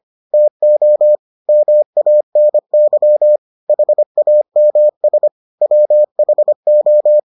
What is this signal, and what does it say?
Morse code, heard as a keyed tone, about 25 wpm, 600 Hz. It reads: TO MANY HAMS WHO